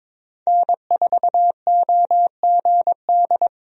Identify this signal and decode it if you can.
Morse code, heard as a keyed tone, about 22 wpm, 700 Hz. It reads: N4OGD
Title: USA Call Signs - 5 Characters - 22wpm